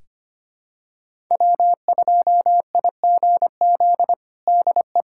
Morse code, heard as a keyed tone, about 25 wpm, 700 Hz. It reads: W2IGZ DE
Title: USA Callsign DE Callsign - 5 Characters - 25wpm